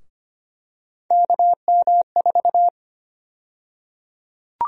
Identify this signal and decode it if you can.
Morse code, heard as a keyed tone, about 25 wpm, 700 Hz. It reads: KM4
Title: USA Call Sign Prefixes - 25wpm